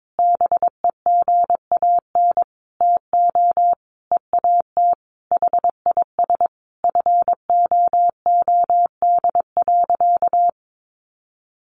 Morse code, heard as a keyed tone, about 22 wpm, 700 Hz. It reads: BEGAN TO EAT HIS FOOD.
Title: Sentences from Top 1100 Words 22wpm